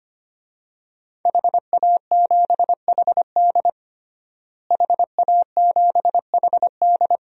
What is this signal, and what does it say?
Morse code, heard as a keyed tone, about 25 wpm, 700 Hz. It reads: HA7HD HA7HD
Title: DXCC-100 Call Signs - 5 Characters - Repeated 3x - 25wpm